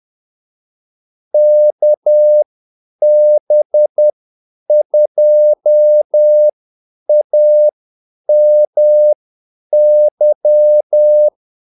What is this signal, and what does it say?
Morse code, heard as a keyed tone, about 10 wpm, spaced as for 8 wpm, 600 Hz. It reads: KB2AMY
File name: Callsigns-10-8wpm